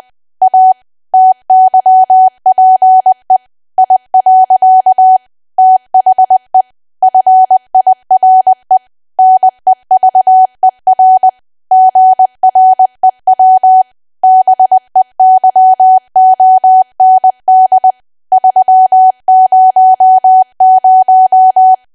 Morse code, heard as a keyed tone, about 20 wpm, 750 Hz. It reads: A TYPE I. THE FIRE NEVER GREW BEYOND 300
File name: Archive / 20 WPM / 20WPM